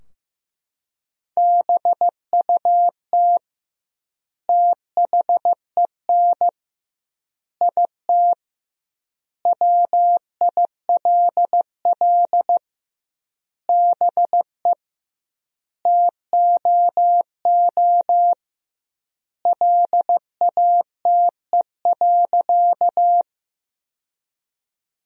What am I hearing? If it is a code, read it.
Morse code, heard as a keyed tone, about 15 wpm, 700 Hz. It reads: BUT THEN IT WILL BE TOO LATE.
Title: 2x Word-Spacing - Sentences from Top 300 Words - 15wpm